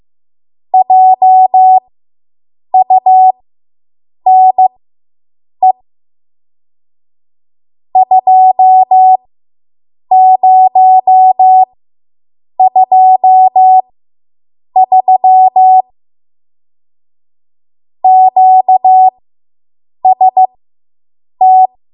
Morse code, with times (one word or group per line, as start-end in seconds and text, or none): 0.7-5.7 JUNE
8.0-15.8 2023
18.0-21.7 QST